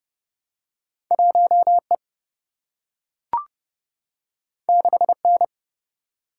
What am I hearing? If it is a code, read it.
Morse code, heard as a keyed tone, about 30 wpm, 700 Hz. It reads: 1 E 6 N
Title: Single-Letter Single-Number 30wpm